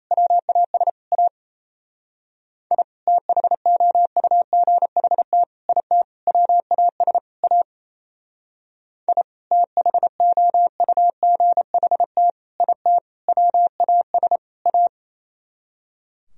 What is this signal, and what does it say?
Morse code, heard as a keyed tone, about 35 wpm, 700 Hz. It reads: WAS A I THOUGHT IT WAS A I THOUGHT IT WAS A
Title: Speed-Racing - Sets of 5 Words 28wpm